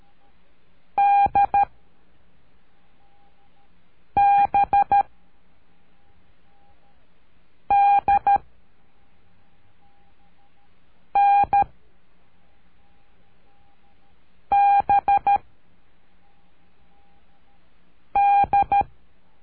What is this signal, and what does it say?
Morse code, heard as a keyed tone, about 13 wpm, 800 Hz. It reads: D B D N B D
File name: cursos / CURSO DE TELEGRAFIA / Telegrafia - Trilha 03